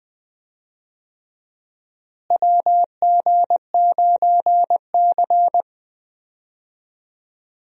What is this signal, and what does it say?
Morse code, heard as a keyed tone, about 21 wpm, 700 Hz. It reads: WG9C
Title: USA Call Signs - 4 Characters - Repeated 3x - 20wpm